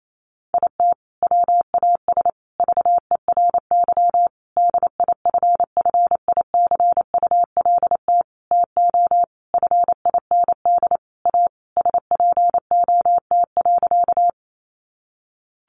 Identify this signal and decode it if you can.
Morse code, heard as a keyed tone, about 28 wpm, 700 Hz. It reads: IT WAS VERY DIFFICULT TO FIND A SPOT.
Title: Sentences from Top 1100 Words 28wpm